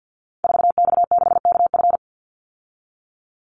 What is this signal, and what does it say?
Morse code, heard as a keyed tone, about 48 wpm, 700 Hz. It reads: 4X6KF
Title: Call Signs Repeated 3x - 50wpm